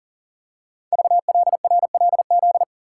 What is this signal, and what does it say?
Morse code, heard as a keyed tone, about 37 wpm, 700 Hz. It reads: VLRLZ
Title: Code Groups - 5 Letters - 40wpm